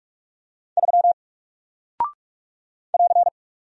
Morse code, heard as a keyed tone, about 44 wpm, 700 Hz. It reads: 3 <AR>
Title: Single Character (Letter:Number:Punctuation:Prosign) Alt 45wpm